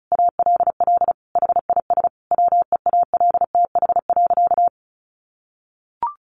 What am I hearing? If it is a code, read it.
Morse code, heard as a keyed tone, about 34 wpm, 700 Hz. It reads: ALL HIS WEALTH.
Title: Sentences from Top 1200 Words 35wpm